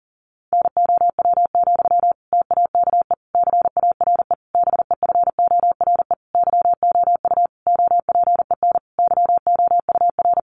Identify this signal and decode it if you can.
Morse code, heard as a keyed tone, about 40 wpm, 700 Hz. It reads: NOW, TAKE CARE BEFORE YOU OPEN YOUR
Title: Sentences from Top 1500 Words 40wpm